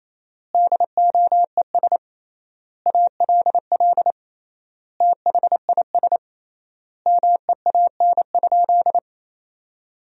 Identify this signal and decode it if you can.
Morse code, heard as a keyed tone, about 27 wpm, 700 Hz. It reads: DOES ALL THIS MEAN?
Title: 3x Word-Spacing - Sentences from Top 200 Words - 28wpm